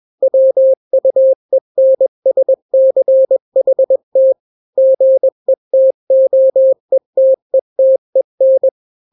Morse code, heard as a keyed tone, about 21 wpm, 530 Hz. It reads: WUENSCHT GETOETETEN
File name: daily / 20wpm